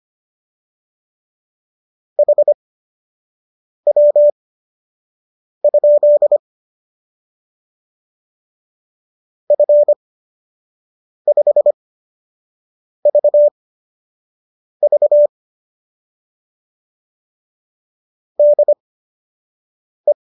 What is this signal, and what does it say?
Morse code, heard as a keyed tone, about 25 wpm, spaced as for 6 wpm, 590 Hz. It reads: HW? F5VV DE